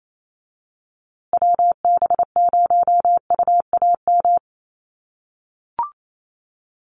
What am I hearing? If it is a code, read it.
Morse code, heard as a keyed tone, about 28 wpm, 700 Hz. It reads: WB0UAM
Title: USA Call Signs Repeated 2x 28wpm